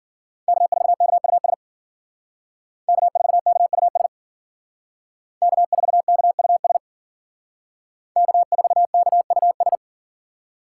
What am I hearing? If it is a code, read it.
Morse code, heard as a keyed tone, about 49 wpm, 700 Hz. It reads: K4KUS K4KUS K4KUS K4KUS
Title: Speed Racing - USA 1x3 Call Signs 40wpm